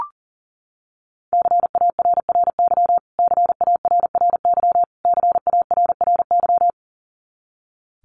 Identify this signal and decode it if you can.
Morse code, heard as a keyed tone, about 40 wpm, 700 Hz. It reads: CARRY CARRY CARRY
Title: Top 500 Words Repeated 40wpm